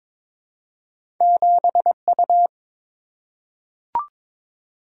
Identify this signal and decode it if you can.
Morse code, heard as a keyed tone, about 22 wpm, 700 Hz. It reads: 7U
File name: practice / 22wpm / Two Letter-Number Combinations - 22wpm